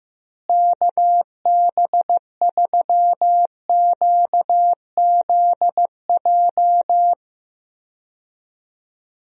Morse code, heard as a keyed tone, about 15 wpm, 700 Hz. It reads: KB3QZJ
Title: USA Call Signs - 6 Characters - Repeated 3x - 15wpm